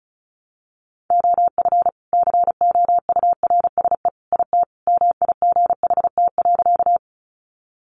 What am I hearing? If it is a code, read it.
Morse code, heard as a keyed tone, about 34 wpm, 700 Hz. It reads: OF COURSE IT MIGHT.
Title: Sentences from Top 500 Words 35wpm